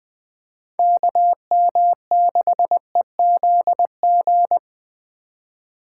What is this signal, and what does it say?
Morse code, heard as a keyed tone, about 25 wpm, 700 Hz. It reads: KM6EZG
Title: Speed Racing - USA 2x3 Call Signs 20wpm